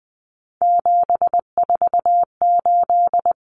Morse code, heard as a keyed tone, about 20 wpm, 700 Hz. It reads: 748
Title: Common Cut Numbers - 3-Digit 20wpm